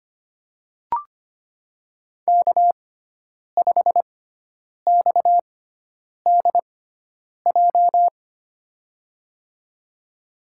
Morse code, heard as a keyed tone, about 25 wpm, spaced as for 9 wpm, 700 Hz. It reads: K5XDJ
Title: USA Call Signs - 6x Character Spacing - 25wpm